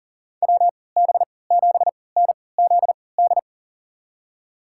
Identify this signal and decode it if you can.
Morse code, heard as a keyed tone, about 40 wpm, spaced as for 23 wpm, 700 Hz. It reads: WB7NZD
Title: USA Call Signs - 3x Character Spacing - 40wpm